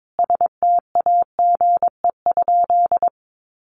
Morse code, heard as a keyed tone, about 22 wpm, 700 Hz. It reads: STAGE?
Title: Sentences from Top 1200 Words 22wpm